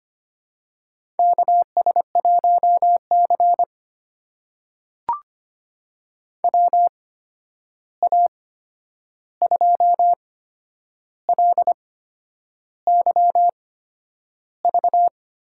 Morse code, heard as a keyed tone, about 25 wpm, spaced as for 9 wpm, 700 Hz. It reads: KS1C WA2LYV
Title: USA Call Signs - 8x Character Spacing - 25wpm